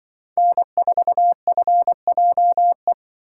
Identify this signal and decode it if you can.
Morse code, heard as a keyed tone, about 24 wpm, 700 Hz. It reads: N4FJE
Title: Speed Racing - USA 1x3 Call Signs 20wpm